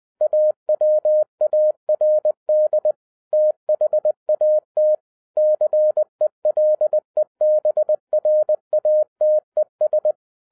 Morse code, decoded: AWARD THAT CELEBRATES

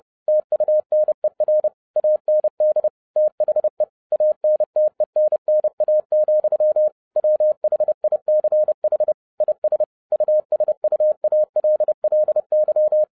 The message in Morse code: TUNER AND THE ANTENNA, WHICH IS USUALLY

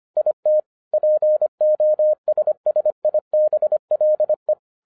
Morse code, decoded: IT POSSIBLE

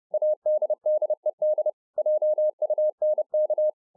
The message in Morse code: ADDED JUNK

